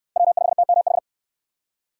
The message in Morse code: W5TMH

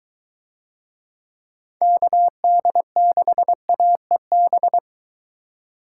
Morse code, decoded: KD6AEB

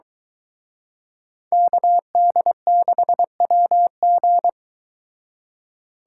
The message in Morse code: KD6WG

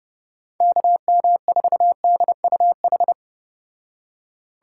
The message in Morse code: KM4DUH

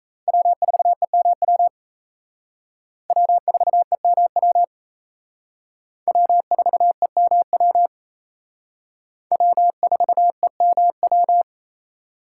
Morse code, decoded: W4EMW W4EMW W4EMW W4EMW

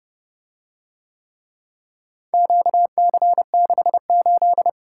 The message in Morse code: QC68

